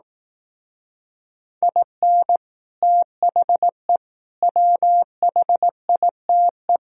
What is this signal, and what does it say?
Morse code, heard as a keyed tone, about 18 wpm, 700 Hz. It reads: IN THE WHITE
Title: Speed-Racing - Sets of 4 Words 15wpm